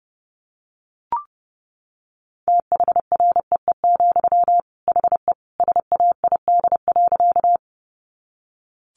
Morse code, heard as a keyed tone, about 30 wpm, 700 Hz. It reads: THREE, HE SAID.
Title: Sentences from Top 200 Words 30wpm